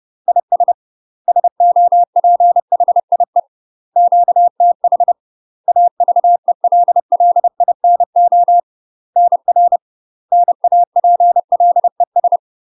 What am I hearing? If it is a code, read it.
Morse code, heard as a keyed tone, about 30 wpm, 700 Hz. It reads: IS SOPHIE QTH AVELLINO NR NAPLES
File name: UK MASTER QSOs TRAINING / 30WPM / UK MASTER QSOs 0011